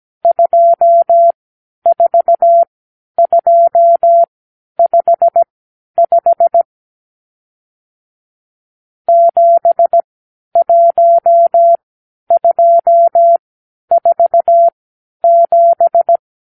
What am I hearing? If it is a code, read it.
Morse code, heard as a keyed tone, about 17 wpm, spaced as for 10 wpm, 680 Hz. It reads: 24255 71247
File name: k7qo / Lesson 54 (9)